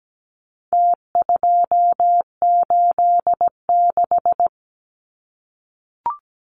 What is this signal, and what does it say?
Morse code, heard as a keyed tone, about 17 wpm, 700 Hz. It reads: T286
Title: Numbers - Contest Style - 4-Digit 17wpm